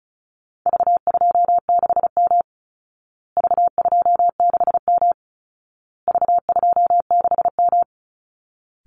Morse code, heard as a keyed tone, about 34 wpm, 700 Hz. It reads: V26M V26M V26M
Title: Call Signs Repeated 2x - 35wpm